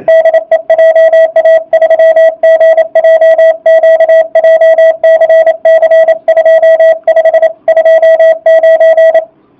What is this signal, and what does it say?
Morse code, heard as a keyed tone, about 27 wpm, 650 Hz. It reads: DE JA3GJQ JCC 2529